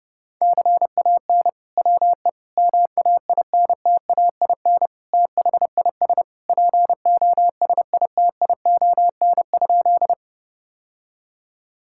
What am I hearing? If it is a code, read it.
Morse code, heard as a keyed tone, about 32 wpm, 700 Hz. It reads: CAN WE MAINTAIN THIS POSITION?
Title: Sentences from Top 1000 Words - Review Part 5 - 30wpm